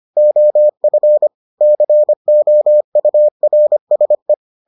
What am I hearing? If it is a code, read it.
Morse code, heard as a keyed tone, about 25 wpm, 600 Hz. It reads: OF COURSE